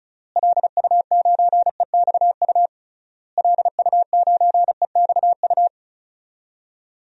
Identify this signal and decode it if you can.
Morse code, heard as a keyed tone, about 36 wpm, 700 Hz. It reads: LU9EXU LU9EXU
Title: DXCC-100 Call Signs - 6 Characters - Repeated 3x - 35wpm